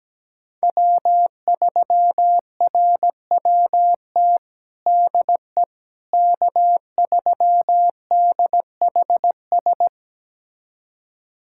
Morse code, heard as a keyed tone, about 17 wpm, 700 Hz. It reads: W3RWT DE K3DHS
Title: USA Callsign DE Callsign - 5 Characters - 17wpm